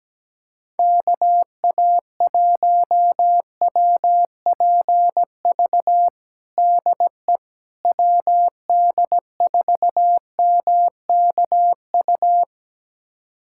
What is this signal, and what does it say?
Morse code, heard as a keyed tone, about 17 wpm, 700 Hz. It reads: KA1WPV DE WD4MKU
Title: USA Callsign DE Callsign - 6 Characters - 17wpm